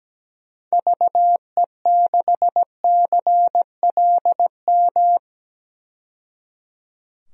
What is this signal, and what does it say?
Morse code, heard as a keyed tone, about 17 wpm, 700 Hz. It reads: VE6CLM
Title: DXCC-100 Call Signs - 6 Characters - 17wpm